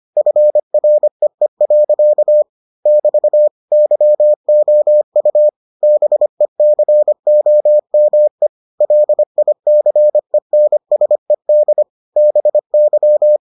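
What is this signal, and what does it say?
Morse code, heard as a keyed tone, about 25 wpm, 600 Hz. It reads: FREE. <BT> YOU BECOME LICENSED BY
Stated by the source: Morse edition of the Foundations of Amateur Radio episodes.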